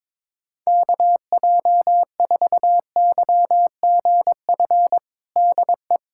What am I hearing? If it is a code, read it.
Morse code, heard as a keyed tone, about 22 wpm, 700 Hz. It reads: KJ4YGF DE
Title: USA Callsign DE Callsign 22wpm